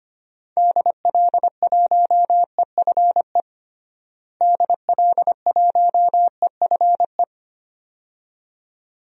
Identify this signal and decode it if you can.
Morse code, heard as a keyed tone, about 25 wpm, 700 Hz. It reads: DL1EFE DL1EFE
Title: DXCC-100 Call Signs - 6 Characters - Repeated 2x - 25wpm